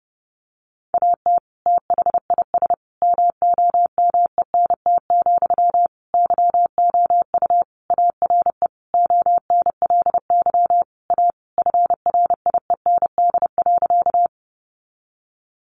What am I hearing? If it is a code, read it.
Morse code, heard as a keyed tone, about 30 wpm, 700 Hz. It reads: AT THIS MOMENT, YOU ARE ONLY A FRIEND.
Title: Sentences from Top 1000 Words 30wpm